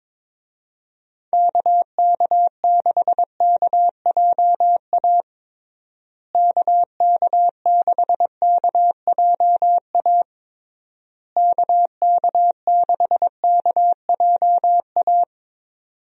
Morse code, heard as a keyed tone, about 22 wpm, 700 Hz. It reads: KK6KJA KK6KJA KK6KJA
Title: USA Call Signs - 6 Characters - Repeated 3x - 22wpm